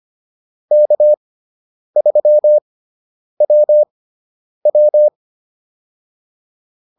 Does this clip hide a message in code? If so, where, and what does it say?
Yes: Morse code from 0.7 to 5.1 s, heard as a keyed tone, about 25 wpm, spaced as for 9 wpm, 600 Hz. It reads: K3WW